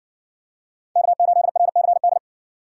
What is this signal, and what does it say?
Morse code, heard as a keyed tone, about 60 wpm, 700 Hz. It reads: K8RCD